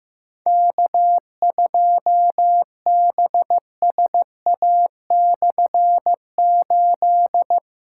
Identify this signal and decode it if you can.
Morse code, heard as a keyed tone, about 15 wpm, 700 Hz. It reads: K2BSA/8
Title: K1USN SST QSO Practice - 15wpm